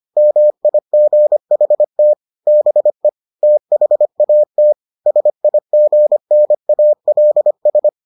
Morse code, heard as a keyed tone, about 25 wpm, 600 Hz. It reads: MIGHT BE THAT SIGNALS